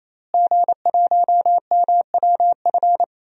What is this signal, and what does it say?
Morse code, heard as a keyed tone, about 28 wpm, 700 Hz. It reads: G1MWF
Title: DXCC-100 Call Signs - 5 Characters - Repeated 3x - 28wpm